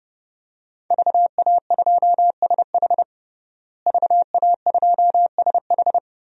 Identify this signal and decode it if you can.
Morse code, heard as a keyed tone, about 30 wpm, 700 Hz. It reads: VA2SH VA2SH
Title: DXCC-100 Call Signs - 5 Characters - Repeated 3x - 30wpm